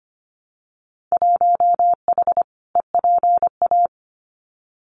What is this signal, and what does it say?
Morse code, heard as a keyed tone, about 25 wpm, 700 Hz. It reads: 1H EPA
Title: Winter Field Day Exchanges 25wpm